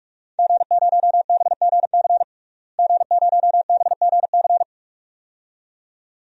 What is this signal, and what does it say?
Morse code, heard as a keyed tone, about 43 wpm, 700 Hz. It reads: G0BGC G0BGC
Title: DXCC-100 Call Signs - 5 Characters - Repeated 2x - 45wpm